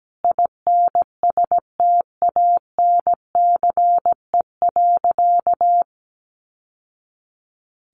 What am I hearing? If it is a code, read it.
Morse code, heard as a keyed tone, about 17 wpm, 700 Hz. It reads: INSTANCE.
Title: Sentences from Top 800 Words 17wpm